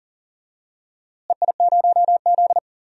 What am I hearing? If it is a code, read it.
Morse code, heard as a keyed tone, about 40 wpm, 700 Hz. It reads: EI0Z
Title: DXCC-100 Call Signs - 4 Characters - Repeated 2x - 40wpm